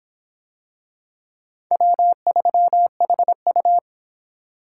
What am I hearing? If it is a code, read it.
Morse code, heard as a keyed tone, about 26 wpm, 700 Hz. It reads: W3HU